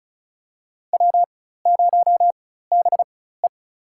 Morse code, heard as a keyed tone, about 36 wpm, spaced as for 16 wpm, 700 Hz. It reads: W0BE
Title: USA Call Signs - 4 Characters - 4x Character Spacing - 35wpm